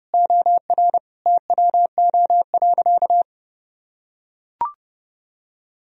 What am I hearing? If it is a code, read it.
Morse code, heard as a keyed tone, about 30 wpm, 700 Hz. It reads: OR TWO.
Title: Sentences from Top 1000 Words - Review Part 1 - 30wpm